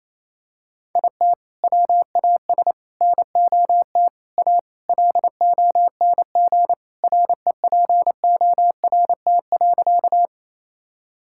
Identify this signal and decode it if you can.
Morse code, heard as a keyed tone, about 28 wpm, 700 Hz. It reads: IT WAS NOT A LONG REPORT.
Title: Sentences from Top 1000 Words - Review Part 6 - 28wpm